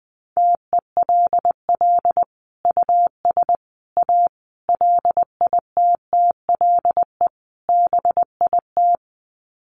Morse code, heard as a keyed tone, about 20 wpm, 700 Hz. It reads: TELL US A LITTLE BIT
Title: Sets of 5 Words Encore2 20wpm